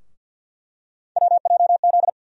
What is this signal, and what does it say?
Morse code, heard as a keyed tone, about 50 wpm, 700 Hz. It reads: WJZ
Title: USA Call Sign Postfixes - 3 Letters 50wpm